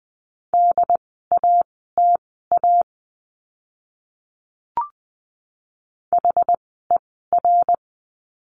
Morse code, heard as a keyed tone, about 20 wpm, spaced as for 15 wpm, 700 Hz. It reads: DATA HER
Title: 101 Lesson 17f - Letter u - Words 20wpm